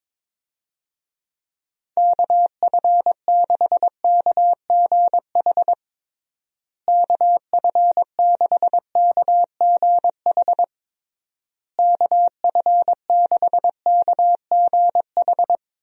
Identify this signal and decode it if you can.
Morse code, heard as a keyed tone, about 22 wpm, 700 Hz. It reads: KF6KGH KF6KGH KF6KGH
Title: USA Call Signs - 6 Characters - Repeated 3x - 22wpm